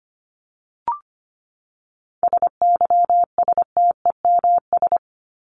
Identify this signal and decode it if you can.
Morse code, heard as a keyed tone, about 25 wpm, 700 Hz. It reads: SYSTEMS
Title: ES:S Suffix - Top 100 Words with Base Words 25wpm